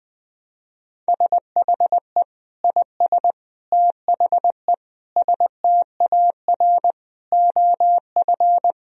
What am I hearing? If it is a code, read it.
Morse code, heard as a keyed tone, about 21 wpm, 700 Hz. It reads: SHE IS THE STAR OF
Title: Sentences from Top 1000 Words - Review Part 2 - 20wpm